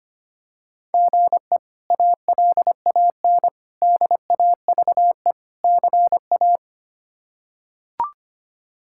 Morse code, heard as a keyed tone, about 25 wpm, 700 Hz. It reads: GE ALAN DAVE CA
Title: K1USN SST QSO Practice - 25wpm